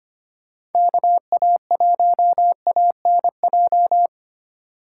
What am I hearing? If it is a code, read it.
Morse code, heard as a keyed tone, about 32 wpm, 700 Hz. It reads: KA1ANJ